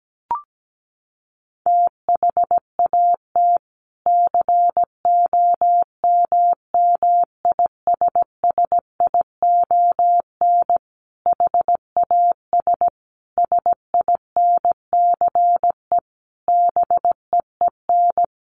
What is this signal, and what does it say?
Morse code, heard as a keyed tone, about 17 wpm, 700 Hz. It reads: THAT COMMISSION HAS SINCE BEEN
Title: Sentences from Top 1400 Words 17wpm